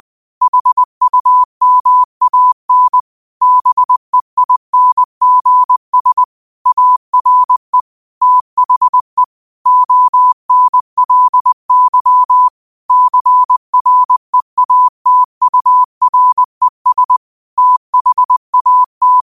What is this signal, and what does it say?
Morse code, heard as a keyed tone, about 20 wpm, 1000 Hz. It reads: HUMAN BEINGS ARE THE ONLY CREATURES THAT
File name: qotdfiles / 20wpm